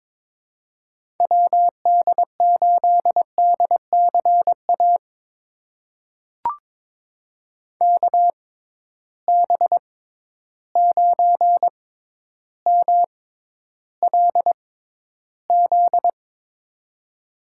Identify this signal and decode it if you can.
Morse code, heard as a keyed tone, about 22 wpm, spaced as for 11 wpm, 700 Hz. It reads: WD8DCA KB9MLZ
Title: USA Call Signs - 6 Characters - 6x Character Spacing - 22wpm